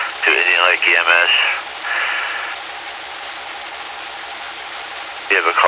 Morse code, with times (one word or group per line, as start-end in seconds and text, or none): none